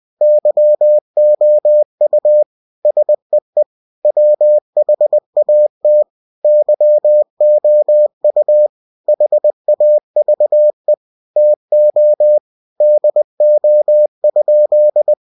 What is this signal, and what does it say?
Morse code, heard as a keyed tone, about 20 wpm, 600 Hz. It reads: YOU SEE WHAT YOU HAVE TO DO?
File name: cw-books / winnie-the-pooh / winnie-20wpm-page-0065